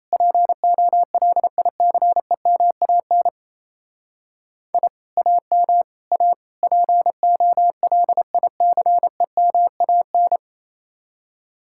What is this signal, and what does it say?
Morse code, heard as a keyed tone, about 34 wpm, 700 Hz. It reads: POLICEMAN I AM A POLICEMAN
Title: Speed-Racing Common QSO Phrases Encore - 28wpm